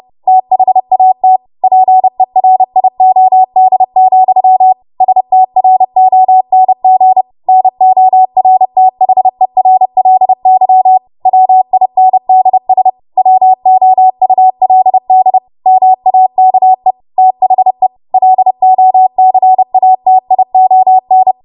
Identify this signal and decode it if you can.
Morse code, heard as a keyed tone, about 30 wpm, 750 Hz. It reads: THAT PERIOD, STRONG NORTHERLY WINDS WOULD MAKE THE LOCATION
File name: Morse / 30WPM